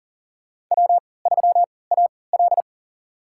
Morse code, heard as a keyed tone, about 40 wpm, spaced as for 23 wpm, 700 Hz. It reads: W3AL
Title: USA Call Signs - 4 Characters - 3x Character Spacing - 40wpm